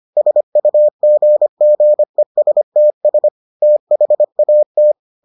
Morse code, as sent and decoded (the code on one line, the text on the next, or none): ... ..- --. --. ... - ... / - .... .- -
SUGGESTS THAT